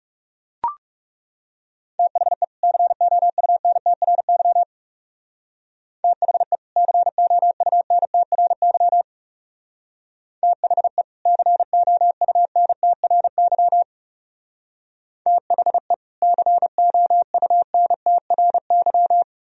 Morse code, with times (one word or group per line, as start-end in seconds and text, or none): 2.0-2.4 THE
2.6-4.6 COUNTRY
6.0-6.6 THE
6.8-9.0 COUNTRY
10.4-11.0 THE
11.3-13.8 COUNTRY
15.3-15.9 THE
16.2-19.2 COUNTRY